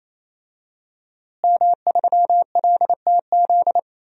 Morse code, heard as a keyed tone, about 28 wpm, 700 Hz. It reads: M3LTZ
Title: DXCC-100 Call Signs - 5 Characters - Repeated 3x - 28wpm